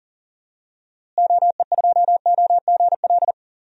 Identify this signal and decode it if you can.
Morse code, heard as a keyed tone, about 40 wpm, 700 Hz. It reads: OE2OGL
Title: DXCC-100 Call Signs - 6 Characters - Repeated 3x - 40wpm